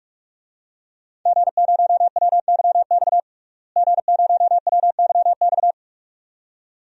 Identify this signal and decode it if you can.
Morse code, heard as a keyed tone, about 44 wpm, 700 Hz. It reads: G0WYX G0WYX
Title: DXCC-100 Call Signs - 5 Characters - Repeated 2x - 45wpm